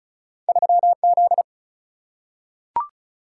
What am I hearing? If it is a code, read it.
Morse code, heard as a keyed tone, about 36 wpm, 700 Hz. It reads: 3Z
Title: Single-Letter Single-Number 35wpm